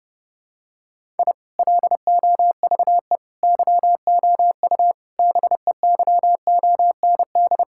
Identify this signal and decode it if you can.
Morse code, heard as a keyed tone, about 30 wpm, 700 Hz. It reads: I LOVE YOU BEYOND
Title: Sentences from Top 1000 Words - Review Part 6 - 30wpm